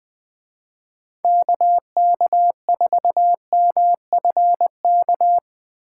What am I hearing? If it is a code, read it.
Morse code, heard as a keyed tone, about 22 wpm, 700 Hz. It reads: KK4MFK
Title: Speed Racing - USA 2x3 Call Signs 20wpm